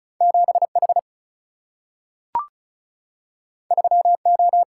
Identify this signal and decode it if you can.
Morse code, heard as a keyed tone, about 36 wpm, 700 Hz. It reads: 7H 3O
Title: Single-Letter Single-Number 35wpm